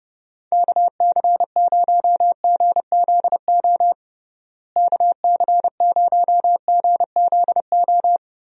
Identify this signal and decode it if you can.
Morse code, heard as a keyed tone, about 30 wpm, 700 Hz. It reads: KC0GZO KC0GZO
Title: USA Call Signs - 6 Characters - Repeated 2x - 30wpm